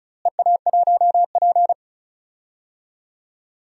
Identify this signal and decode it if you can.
Morse code, heard as a keyed tone, about 34 wpm, 700 Hz. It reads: EA1P
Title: DXCC-100 Call Signs - 4 Characters - Repeated 3x - 35wpm